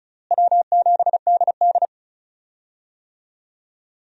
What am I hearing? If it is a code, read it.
Morse code, heard as a keyed tone, about 36 wpm, 700 Hz. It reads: W7DD
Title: USA Call Signs - 4 Characters - Repeated 3x - 35wpm